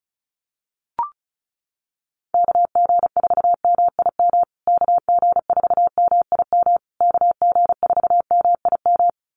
Morse code, heard as a keyed tone, about 34 wpm, 700 Hz. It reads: KG4MIM KG4MIM KG4MIM
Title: USA Call Signs Repeated 3x 35wpm